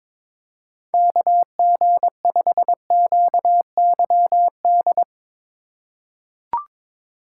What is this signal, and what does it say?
Morse code, heard as a keyed tone, about 22 wpm, 700 Hz. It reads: KG5QYD
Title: USA Call Signs - 6 Characters - 4x Character Spacing - 22wpm